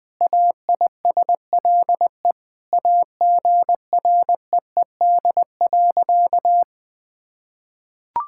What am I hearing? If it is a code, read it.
Morse code, heard as a keyed tone, about 20 wpm, 700 Hz. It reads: AISLE AGREED.
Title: Sentences with 6-Word Phrases - Spoken Beginning - 20wpm